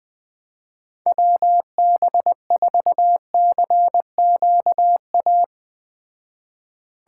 Morse code, heard as a keyed tone, about 20 wpm, 700 Hz. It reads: WB4CQA